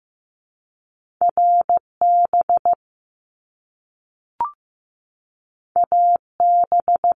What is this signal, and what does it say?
Morse code, heard as a keyed tone, about 15 wpm, 700 Hz. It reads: RB AB
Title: Top Two Letter Combinations in Words 15wpm